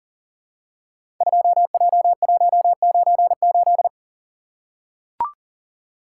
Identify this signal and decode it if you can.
Morse code, heard as a keyed tone, about 39 wpm, 700 Hz. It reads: 2J198
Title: Code Groups - 5 Letter:Numbers - 40wpm